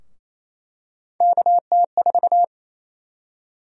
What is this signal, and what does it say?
Morse code, heard as a keyed tone, about 28 wpm, 700 Hz. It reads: KT4
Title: USA Call Sign Prefixes - Repeated 2x - 28wpm